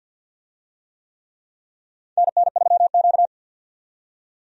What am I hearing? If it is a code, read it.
Morse code, heard as a keyed tone, about 50 wpm, 700 Hz. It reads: NN3Q
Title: USA Call Signs - 4 Characters - 50wpm